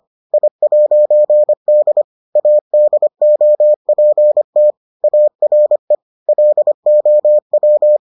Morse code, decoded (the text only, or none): I'D ADOPT ARE LOW